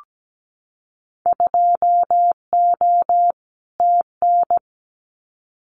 Morse code, heard as a keyed tone, about 17 wpm, 700 Hz. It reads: 2O TN
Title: Winter Field Day Exchanges 17wpm